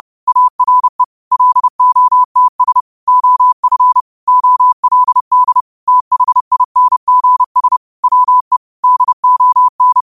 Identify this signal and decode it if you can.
Morse code, heard as a keyed tone, about 30 wpm, 1000 Hz. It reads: ARE LOTS OF OLD THINGS WE DON